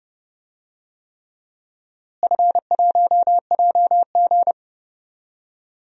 Morse code, heard as a keyed tone, about 32 wpm, 700 Hz. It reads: F1JG
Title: DXCC-100 Call Signs - 4 Characters - Repeated 3x - 30wpm